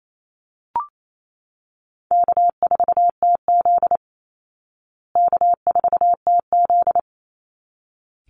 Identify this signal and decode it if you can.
Morse code, heard as a keyed tone, about 28 wpm, 700 Hz. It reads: K4TZ K4TZ
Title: Call Signs Repeated 3x - 28wpm